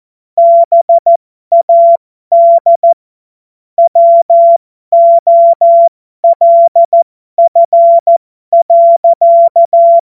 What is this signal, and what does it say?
Morse code, heard as a keyed tone, about 14 wpm, spaced as for 12 wpm, 680 Hz. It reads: BAD WOLF.
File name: Morsekurs / Witze